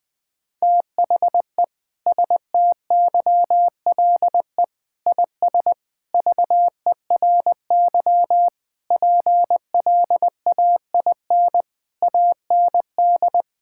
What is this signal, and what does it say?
Morse code, heard as a keyed tone, about 20 wpm, 700 Hz. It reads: THE STYLE IS VERY PLAIN AND
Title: Sentences from Top 1000 Words - Review Part 5 - 20wpm